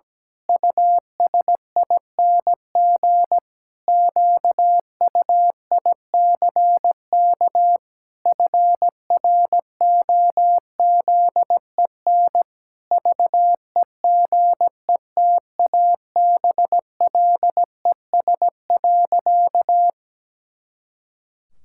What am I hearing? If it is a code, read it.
Morse code, heard as a keyed tone, about 17 wpm, 700 Hz. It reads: USING QUICK FROZEN VEGETABLES.
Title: Pangram Challenge v2 - 17wpm